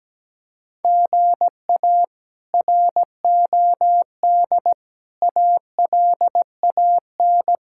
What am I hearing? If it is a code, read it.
Morse code, heard as a keyed tone, about 17 wpm, 700 Hz. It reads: GA ROD ALAN
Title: K1USN SST QSO Practice - 17wpm